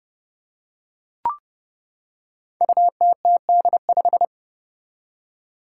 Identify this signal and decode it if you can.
Morse code, heard as a keyed tone, about 31 wpm, 700 Hz. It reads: UTTD5
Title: Five Challenging Letter-Number Combinations 30wpm